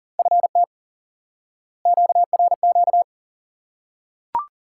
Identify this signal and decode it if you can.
Morse code, heard as a keyed tone, about 40 wpm, 700 Hz. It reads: FT QRQ
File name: IQE Course 3 - Lesson 89 - YAGI - Two Elements - 40wpm